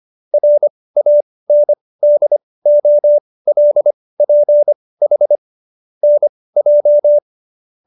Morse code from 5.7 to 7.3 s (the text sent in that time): NJ